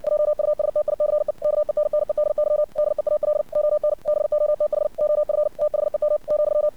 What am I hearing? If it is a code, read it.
Morse code, heard as a keyed tone, about 60 wpm, 600 Hz. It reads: JUSTICE KENNEDY DEAL ON BOTH OF THEM ,
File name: archived / CRSnet